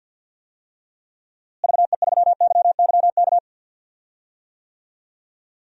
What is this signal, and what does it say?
Morse code, heard as a keyed tone, about 49 wpm, 700 Hz. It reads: VE3YYK
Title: DXCC-100 Call Signs - 6 Characters - Repeated 3x - 50wpm